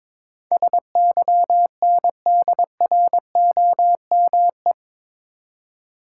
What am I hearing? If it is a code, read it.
Morse code, heard as a keyed tone, about 22 wpm, 700 Hz. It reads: SYNDROME
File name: practice / 22wpm / Eight Letter Words - 22wpm